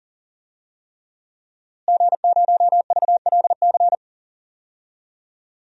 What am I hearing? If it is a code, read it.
Morse code, heard as a keyed tone, about 40 wpm, 700 Hz. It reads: G0VLC
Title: DXCC-100 Call Signs - 5 Characters - Repeated 3x - 40wpm